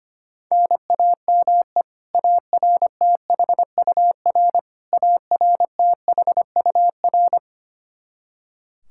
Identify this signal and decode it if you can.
Morse code, heard as a keyed tone, about 25 wpm, 700 Hz. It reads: NAME ARTHUR ARTHUR
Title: Practice QSO - Exchange 1 - Beginning - 25wpm